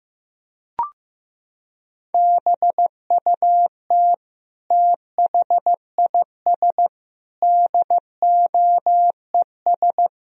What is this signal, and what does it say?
Morse code, heard as a keyed tone, about 15 wpm, 700 Hz. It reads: BUT THIS DOES
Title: Sentences from Top 1000 Words - Review Part 2 - 15wpm